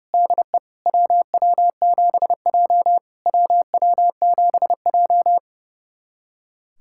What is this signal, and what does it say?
Morse code, heard as a keyed tone, about 30 wpm, 690 Hz. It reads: DE WW7J WW7J
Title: SOTA QSO Practice 30wpm